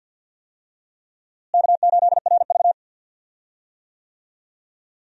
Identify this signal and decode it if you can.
Morse code, heard as a keyed tone, about 50 wpm, 700 Hz. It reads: K8RV
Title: USA Call Signs - 4 Characters - Repeated 2x - 50wpm